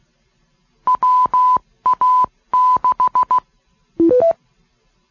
Morse code, heard as a keyed tone, about 16 wpm, 1000 Hz. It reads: WA6